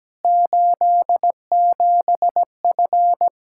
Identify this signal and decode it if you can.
Morse code, heard as a keyed tone, about 17 wpm, 700 Hz. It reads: 87F
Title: Practice QSO - Exchange 2 - Info - 17wpm